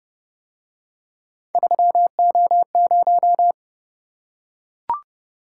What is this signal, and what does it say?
Morse code, heard as a keyed tone, about 29 wpm, 700 Hz. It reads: 3O0